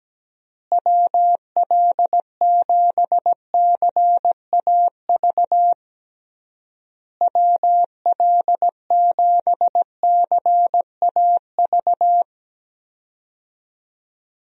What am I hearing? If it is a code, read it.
Morse code, heard as a keyed tone, about 17 wpm, 700 Hz. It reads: WL7CAV WL7CAV
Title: USA Call Signs - 6 Characters - Repeated 3x - 17wpm